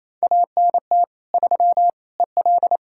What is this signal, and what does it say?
Morse code, heard as a keyed tone, about 38 wpm, 700 Hz. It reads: ANT 3 EL